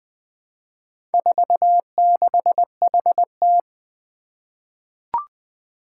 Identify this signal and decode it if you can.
Morse code, heard as a keyed tone, about 20 wpm, 700 Hz. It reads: 46HT
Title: Four Challenging Letter-Number Combinations 20wpm